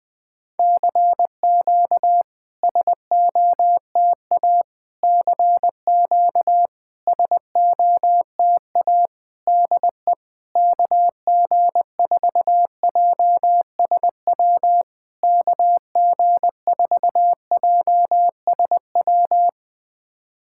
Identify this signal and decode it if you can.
Morse code, heard as a keyed tone, about 20 wpm, 690 Hz. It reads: CQ SOTA CQ SOTA DE KG4JSW KG4JSW
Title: SOTA QSO Practice 20wpm